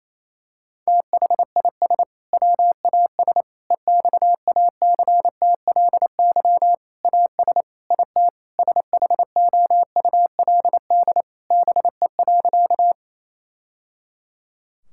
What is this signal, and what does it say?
Morse code, heard as a keyed tone, about 28 wpm, 700 Hz. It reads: THIS WAS EXACTLY AS IT SHOULD BE.
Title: Sentences from Top 1000 Words - Review Part 6 - 28wpm